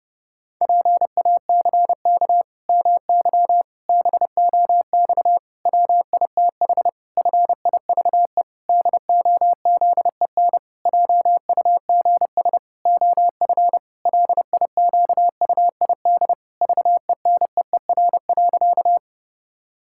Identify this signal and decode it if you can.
Morse code, heard as a keyed tone, about 32 wpm, 700 Hz. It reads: PACK MY BOX WITH FIVE DOZEN JUGS OF LIQUID VENEER.
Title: Pangram Challenge v2 - 30wpm